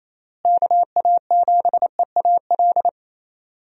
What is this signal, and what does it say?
Morse code, heard as a keyed tone, about 27 wpm, 700 Hz. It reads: KA7EAL
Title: USA Call Signs - 6 Characters - Repeated 3x - 28wpm